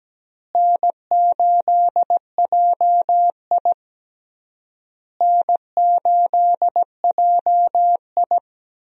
Morse code, heard as a keyed tone, about 17 wpm, 700 Hz. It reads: N8JI N8JI
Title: USA Call Signs - 4 Characters - Repeated 3x - 17wpm